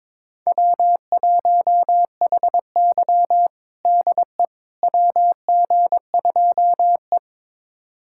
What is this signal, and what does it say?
Morse code, heard as a keyed tone, about 22 wpm, 700 Hz. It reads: W1HY DE WG2E
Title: USA Callsign DE Callsign - 4 Characters - 22wpm